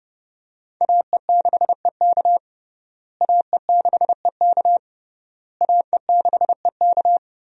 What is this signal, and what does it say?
Morse code, heard as a keyed tone, about 30 wpm, 700 Hz. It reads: AE6EK AE6EK AE6EK
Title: USA Call Signs - 5 Characters - Repeated 3x - 30wpm